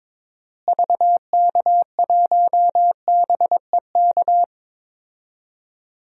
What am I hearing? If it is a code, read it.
Morse code, heard as a keyed tone, about 22 wpm, 700 Hz. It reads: VK1BEK
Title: DXCC-100 Call Signs - 6 Characters - Repeated 3x - 22wpm